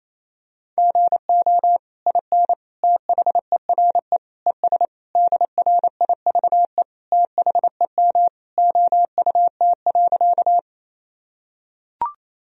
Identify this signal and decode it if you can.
Morse code, heard as a keyed tone, about 28 wpm, 700 Hz. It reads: GO IN THERE ES DRIVE THEM OUT.
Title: Sentences from Top 500 Words and Abbreviations 28wpm